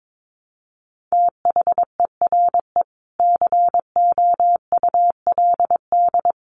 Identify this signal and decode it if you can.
Morse code, heard as a keyed tone, about 22 wpm, 700 Hz. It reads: THERE COULD
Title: Sentences from Top 700 Words 22wpm